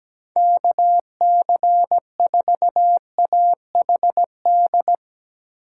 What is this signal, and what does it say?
Morse code, heard as a keyed tone, about 17 wpm, 700 Hz. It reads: KC4AHD
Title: USA Call Signs - 6 Characters - 8x Character Spacing - 17wpm